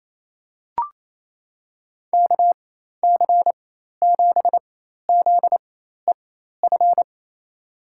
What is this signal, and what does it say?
Morse code, heard as a keyed tone, about 28 wpm, spaced as for 13 wpm, 700 Hz. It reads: KC7ZEF
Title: USA Call Signs - 6 Characters - 4x Character Spacing - 28wpm